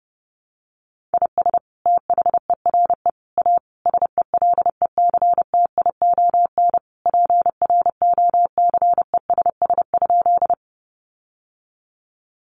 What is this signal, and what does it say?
Morse code, heard as a keyed tone, about 30 wpm, 700 Hz. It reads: IS THERE A SELECTION PROCESS?
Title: Sentences from Top 1400 Words 30wpm